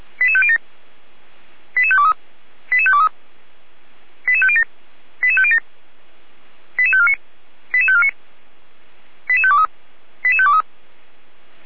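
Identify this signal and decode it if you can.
Five-tone selective call (ZVEI), heard as quick runs of five tones, tones of about 70 ms. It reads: 88507 88523 88523 88507 88507 88549 88549 88523 88523
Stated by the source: ZVEI five-tone call 88505 received by SDR on channel 455.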